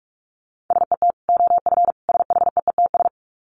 Morse code, heard as a keyed tone, about 44 wpm, 700 Hz. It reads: SET OF SHEETS
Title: Dit Challenge 45wpm